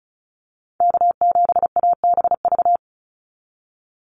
Morse code, heard as a keyed tone, about 34 wpm, 700 Hz. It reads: K7ABV
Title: Call Signs Repeated 3x - 35wpm